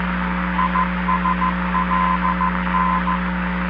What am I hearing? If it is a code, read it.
Morse code, heard as a keyed tone, about 14 wpm, 1020 Hz, with noise. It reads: ISLN